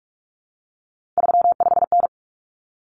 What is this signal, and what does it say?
Morse code, heard as a keyed tone, about 44 wpm, 700 Hz. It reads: 35N
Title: Common Cut Numbers - 3-Digit 45wpm